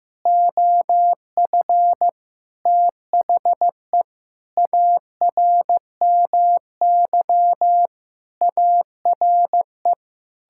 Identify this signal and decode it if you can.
Morse code, heard as a keyed tone, about 15 wpm, 700 Hz. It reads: OF THE ARMY ARE
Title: Sentences from Top 1000 Words - Review Part 6 - 15wpm